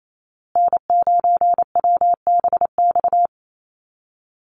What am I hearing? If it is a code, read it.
Morse code, heard as a keyed tone, about 28 wpm, 700 Hz. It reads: N9WBX
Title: USA Call Signs Repeated 2x 28wpm